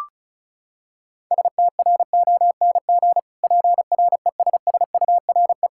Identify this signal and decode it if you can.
Morse code, heard as a keyed tone, about 36 wpm, 700 Hz. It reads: STRONG PRESSURE
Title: Sentences from Top 1000 Words - Review Part 6 - 35wpm